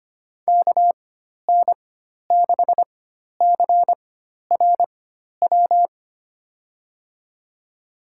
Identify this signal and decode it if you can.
Morse code, heard as a keyed tone, about 25 wpm, spaced as for 12 wpm, 700 Hz. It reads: KN6CRW
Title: USA Call Signs - 4x Character Spacing - 25wpm